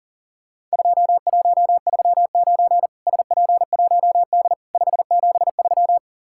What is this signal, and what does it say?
Morse code, heard as a keyed tone, about 40 wpm, 700 Hz. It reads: 2139 SP1D 573
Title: Worked All Europe DX Contest - QTC Exchange - 40wpm